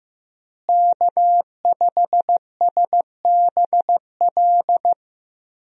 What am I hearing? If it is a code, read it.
Morse code, heard as a keyed tone, about 15 wpm, 700 Hz. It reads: K5SBL
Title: USA Call Signs - 5 Characters - Repeated 2x - 15wpm